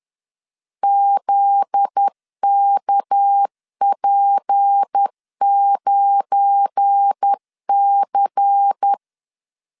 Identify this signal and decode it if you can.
Morse code, heard as a keyed tone, about 10 wpm, 800 Hz. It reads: ZKP9C
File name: CW / DET / 1 9 1500C S B 10wpm 4439